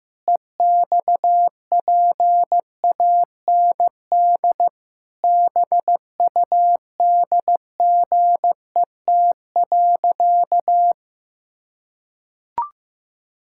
Morse code, heard as a keyed tone, about 15 wpm, 700 Hz. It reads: EXPAND BUDGET.
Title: Pangram Challenge v2 - 15wpm